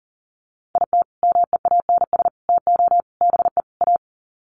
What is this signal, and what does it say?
Morse code, heard as a keyed tone, about 40 wpm, 700 Hz. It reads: IT MEANS TO BE A
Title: Sets of 5 Words Encore2 40wpm